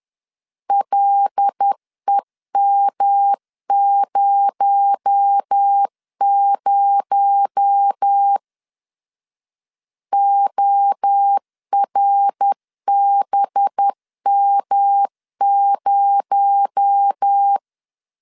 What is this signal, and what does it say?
Morse code, heard as a keyed tone, about 10 wpm, 800 Hz. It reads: LEM00 ORBM0
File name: LEZIONI / 6 PKX0 / 6 PKX0 10wpm 2707